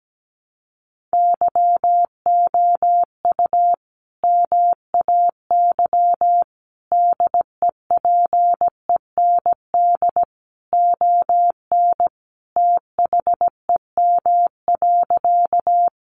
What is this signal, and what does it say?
Morse code, heard as a keyed tone, about 17 wpm, 700 Hz. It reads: YOU MAY DEPEND ON THEM.
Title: Sentences from Top 1300 Words 17wpm